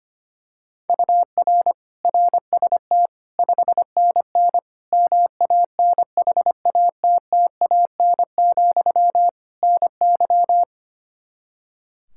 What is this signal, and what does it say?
Morse code, heard as a keyed tone, about 25 wpm, 700 Hz. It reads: UR RST 5NN MANHATTAN, NY
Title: Practice QSO - Exchange 1 - Beginning - 25wpm